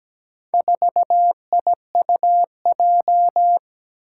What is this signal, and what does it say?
Morse code, heard as a keyed tone, about 17 wpm, 700 Hz. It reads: 4IUJ
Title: Four Challenging Letter-Number Combinations 17wpm